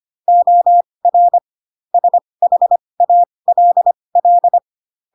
Morse code, heard as a keyed tone, about 25 wpm, spaced as for 20 wpm, 700 Hz. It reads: OR SHALL